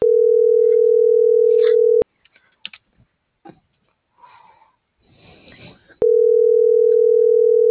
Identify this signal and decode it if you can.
Ringback tone (North America), 440 + 480 Hz.